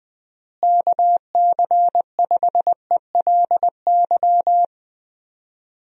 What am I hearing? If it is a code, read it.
Morse code, heard as a keyed tone, about 20 wpm, 700 Hz. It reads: KC5ELY
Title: USA Call Signs - 6 Characters - Repeated 2x - 20wpm